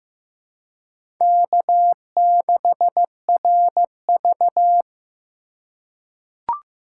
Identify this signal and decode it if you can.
Morse code, heard as a keyed tone, about 15 wpm, 700 Hz. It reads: K6RV
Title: USA Call Signs - 4 Characters - 4x Character Spacing - 15wpm